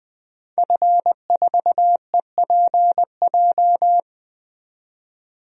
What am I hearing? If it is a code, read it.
Morse code, heard as a keyed tone, about 20 wpm, 700 Hz. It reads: F4EPJ
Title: DXCC-100 Call Signs - 5 Characters - 20wpm